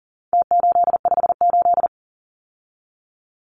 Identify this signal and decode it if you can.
Morse code, heard as a keyed tone, about 40 wpm, 700 Hz. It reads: T858
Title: Numbers - Contest Style - 4-Digit 40wpm